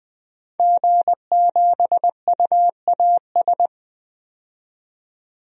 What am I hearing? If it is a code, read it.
Morse code, heard as a keyed tone, about 20 wpm, 700 Hz. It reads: G7UAS
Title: DXCC-100 Call Signs - 5 Characters - 20wpm